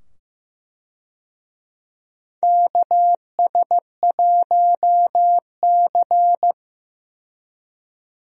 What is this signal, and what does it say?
Morse code, heard as a keyed tone, about 15 wpm, 700 Hz. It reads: KS1C
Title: USA Call Signs - 4 Characters - 15wpm